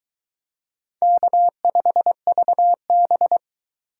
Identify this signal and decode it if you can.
Morse code, heard as a keyed tone, about 22 wpm, 700 Hz. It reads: K5VB